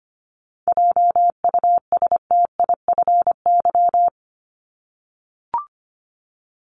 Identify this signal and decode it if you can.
Morse code, heard as a keyed tone, about 25 wpm, 700 Hz. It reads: JUSTIFY
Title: Top 1751-2000 Words 25wpm